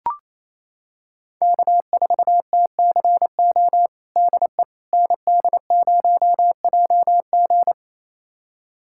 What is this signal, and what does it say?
Morse code, heard as a keyed tone, about 28 wpm, 700 Hz. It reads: K4TCO DE ND0JG
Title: USA Callsign DE Callsign - 5 Characters - 28wpm